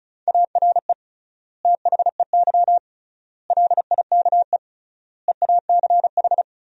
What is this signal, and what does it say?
Morse code, heard as a keyed tone, about 36 wpm, 700 Hz. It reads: ARE THEY LIKE EACH
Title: 3x Word-Spacing - Sentences from Top 200 Words - 35wpm